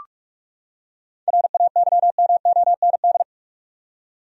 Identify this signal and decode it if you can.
Morse code, heard as a keyed tone, about 46 wpm, 700 Hz. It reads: RAYMOND
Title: USA Common Names - 45wpm